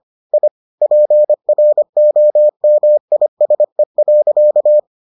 Morse code, heard as a keyed tone, about 25 wpm, 600 Hz. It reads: I PROMISE.